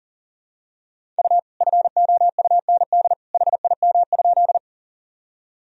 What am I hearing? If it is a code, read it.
Morse code, heard as a keyed tone, about 40 wpm, 700 Hz. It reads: U FOUND HIM?
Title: Sentences from Top 300 Words and Abbreviations 40wpm